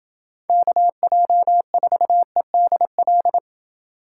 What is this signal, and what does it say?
Morse code, heard as a keyed tone, about 27 wpm, 700 Hz. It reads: KJ4EDL